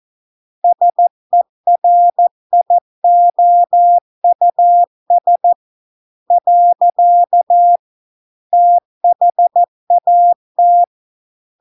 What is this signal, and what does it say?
Morse code, heard as a keyed tone, about 14 wpm, 700 Hz. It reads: SERIOUS . THAT